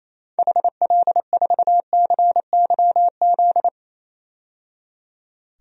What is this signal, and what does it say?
Morse code, heard as a keyed tone, about 28 wpm, 700 Hz. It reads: HL4CYZ
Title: DXCC-100 Call Signs - 6 Characters - 28wpm